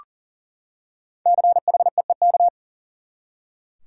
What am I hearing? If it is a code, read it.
Morse code, heard as a keyed tone, about 40 wpm, 700 Hz. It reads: CHEEK
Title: Five Letter Words - 40wpm